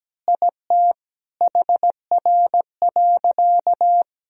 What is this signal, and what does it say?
Morse code, heard as a keyed tone, about 17 wpm, 700 Hz. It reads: IT HR.
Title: Sentences from Top 400 Words and Abbreviations 17wpm